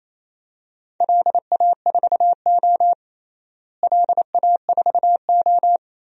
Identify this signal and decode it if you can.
Morse code, heard as a keyed tone, about 28 wpm, 700 Hz. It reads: LA4O LA4O
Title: DXCC-100 Call Signs - 4 Characters - Repeated 3x - 28wpm